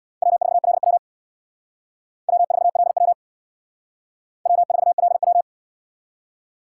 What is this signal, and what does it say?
Morse code, heard as a keyed tone, about 64 wpm, 700 Hz. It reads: W4LW W4LW W4LW